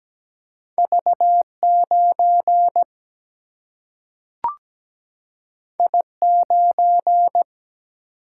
Morse code, heard as a keyed tone, about 17 wpm, 700 Hz. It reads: V9 I9
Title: Single-Letter Single-Number 17wpm